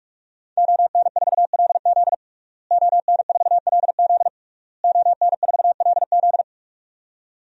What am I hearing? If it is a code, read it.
Morse code, heard as a keyed tone, about 46 wpm, 700 Hz. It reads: ON4LZ ON4LZ ON4LZ
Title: DXCC-100 Call Signs - 5 Characters - Repeated 3x - 45wpm